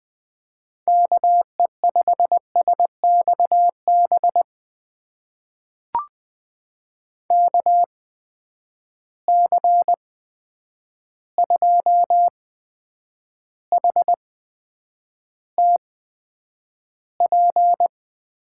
Morse code, heard as a keyed tone, about 20 wpm, spaced as for 9 wpm, 700 Hz. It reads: KE5SXB KC2HTP
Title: USA Call Signs - 6 Characters - 8x Character Spacing - 20wpm